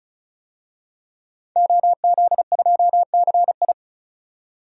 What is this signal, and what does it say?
Morse code, heard as a keyed tone, about 36 wpm, 700 Hz. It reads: OZ2CI
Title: DXCC-100 Call Signs - 5 Characters - 35wpm